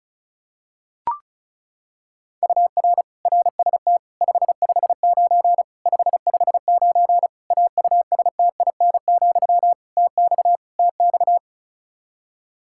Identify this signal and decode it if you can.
Morse code, heard as a keyed tone, about 36 wpm, 700 Hz. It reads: UR RST 559 559 AUSTIN, TX TX
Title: K1USN SST QSO Practice 35wpm